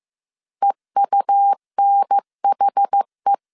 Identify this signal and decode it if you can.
Morse code, heard as a keyed tone, about 15 wpm, 800 Hz. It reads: EUNHE